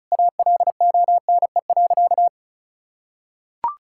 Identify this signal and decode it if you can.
Morse code, heard as a keyed tone, about 36 wpm, 700 Hz. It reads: ALONE.
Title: Sentences from Top 1000 Words - Review Part 5 - 35wpm